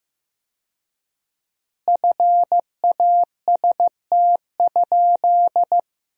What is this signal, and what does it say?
Morse code, heard as a keyed tone, about 15 wpm, 700 Hz. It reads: FAST?
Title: 4x Word-Spacing - Sentences from Top 400 Words - 15wpm